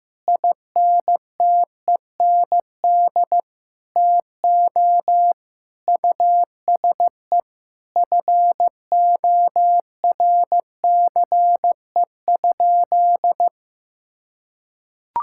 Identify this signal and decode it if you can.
Morse code, heard as a keyed tone, about 15 wpm, 700 Hz. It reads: INTEND TO USE FORCE?
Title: Sentences from Top 1600 Words 15wpm